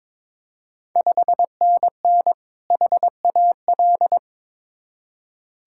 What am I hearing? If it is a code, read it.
Morse code, heard as a keyed tone, about 21 wpm, 690 Hz. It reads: 5NN HAL
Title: Ontario QSO Party Exchange 22wpm